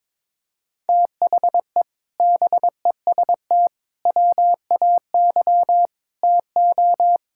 Morse code, heard as a keyed tone, about 22 wpm, 700 Hz. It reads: THE BEST WAY TO
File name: practice / 22wpm / Sentences with 4-Word Phrases - 22wpm